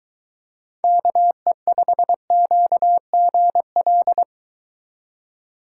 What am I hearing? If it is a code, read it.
Morse code, heard as a keyed tone, about 23 wpm, 700 Hz. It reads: KE5QGL